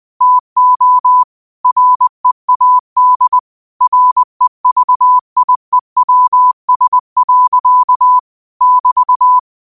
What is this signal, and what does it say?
Morse code, heard as a keyed tone, about 20 wpm, 1000 Hz. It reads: TO READ REVIEWS. <BT>
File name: qotdfiles / 20wpm